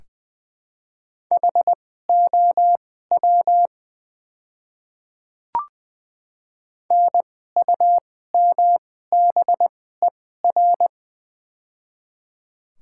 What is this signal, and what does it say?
Morse code, heard as a keyed tone, about 20 wpm, spaced as for 14 wpm, 700 Hz. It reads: HOW NUMBER
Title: 2x Character Spacing - Top 100 Words - 20wpm